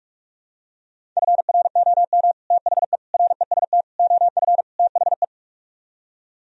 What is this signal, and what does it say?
Morse code, heard as a keyed tone, about 45 wpm, 700 Hz. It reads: FROM THE REST OF THE